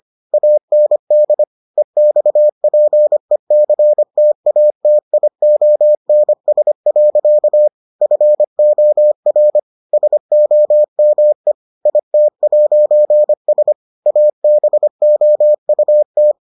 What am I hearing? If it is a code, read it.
Morse code, heard as a keyed tone, about 25 wpm, 600 Hz. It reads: AND EXPECTATIONS. FOR SOME IT'S ABOUT